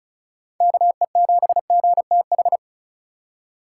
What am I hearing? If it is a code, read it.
Morse code, heard as a keyed tone, about 36 wpm, 700 Hz. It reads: KE7GTH
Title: USA Call Signs - 6 Characters - 3x Character Spacing - 35wpm